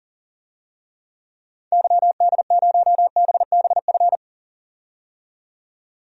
Morse code, heard as a keyed tone, about 40 wpm, 700 Hz. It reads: YD0BBF
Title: DXCC-100 Call Signs - 6 Characters - 40wpm